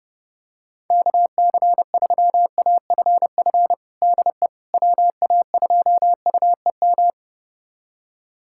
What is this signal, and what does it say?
Morse code, heard as a keyed tone, about 30 wpm, 700 Hz. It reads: KC3AFF DE WA2UEM
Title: USA Callsign DE Callsign - 6 Characters - 30wpm